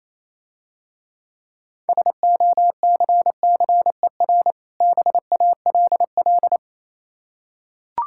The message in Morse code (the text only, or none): SOCCER BALL